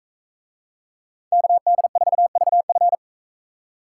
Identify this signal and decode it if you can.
Morse code, heard as a keyed tone, about 42 wpm, 700 Hz. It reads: KD4VF